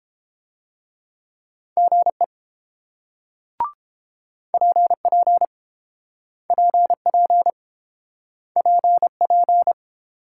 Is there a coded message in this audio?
Yes: Morse code from 1.8 to 9.7 s, heard as a keyed tone, about 28 wpm, 700 Hz. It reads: GE PP PP PP